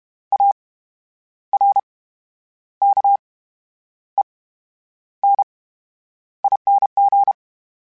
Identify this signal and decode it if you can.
Morse code, heard as a keyed tone, about 32 wpm, 800 Hz. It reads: A R K E N I N G
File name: Special INGs / Special INGs-mp3s / VerySlowToFaster Ing @30